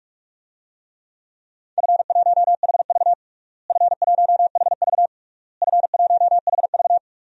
Morse code, heard as a keyed tone, about 46 wpm, 700 Hz. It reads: F1HV F1HV F1HV
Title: DXCC-100 Call Signs - 4 Characters - Repeated 3x - 45wpm